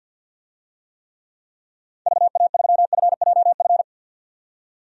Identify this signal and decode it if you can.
Morse code, heard as a keyed tone, about 52 wpm, 700 Hz. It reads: VA3FJF
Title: DXCC-100 Call Signs - 6 Characters - 50wpm